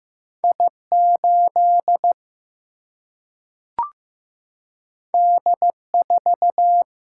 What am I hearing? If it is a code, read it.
Morse code, heard as a keyed tone, about 15 wpm, 700 Hz. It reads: I 8 D 4
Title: Single-Letter Single-Number 15wpm